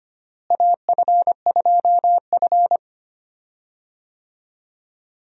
Morse code, heard as a keyed tone, about 25 wpm, 700 Hz. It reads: AF2F
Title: USA Call Signs - 4 Characters - Repeated 3x - 25wpm